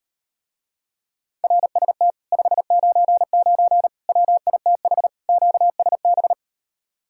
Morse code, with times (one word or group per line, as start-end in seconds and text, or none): 1.4-2.1 RST
2.3-3.9 599
4.1-5.1 WITH
5.3-6.3 QSB